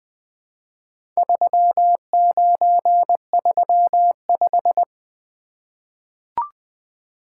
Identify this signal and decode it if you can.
Morse code, heard as a keyed tone, about 20 wpm, 700 Hz. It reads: 3935
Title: Four Letter-Number Combinations - 20wpm